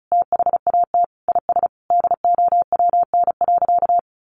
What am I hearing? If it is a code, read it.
Morse code, heard as a keyed tone, about 34 wpm, 700 Hz. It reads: THAT IS DOWN.
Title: Sentences from Top 1300 Words 35wpm